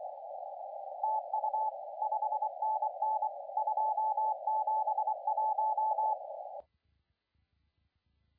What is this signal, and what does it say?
Morse code, heard as a keyed tone, about 24 wpm, 860 Hz, with radio noise. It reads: TU 5NN 271
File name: la4c / sac-cw-2017 / CONTEST AUDIO 1100Z